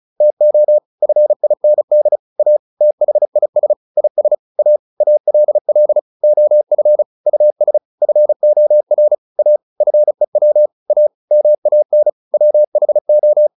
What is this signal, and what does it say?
Morse code, heard as a keyed tone, about 34 wpm, 600 Hz. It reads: TO FIND A THIS IS A ALL OF US FOR A FEW A MAN WHO